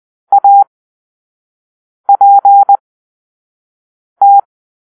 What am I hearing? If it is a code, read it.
Morse code, heard as a keyed tone, about 20 wpm, spaced as for 5 wpm, 800 Hz. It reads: APT
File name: arrl / Grp-6 Drill-3